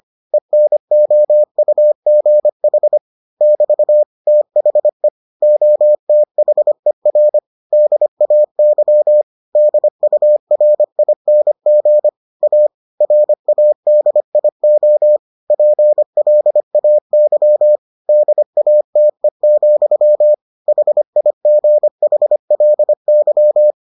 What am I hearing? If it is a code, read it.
Morse code, heard as a keyed tone, about 25 wpm, 600 Hz. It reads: ENOUGH <BT> THE OTHER DAY DURING A RADIO PLAY DATE, HIGHLY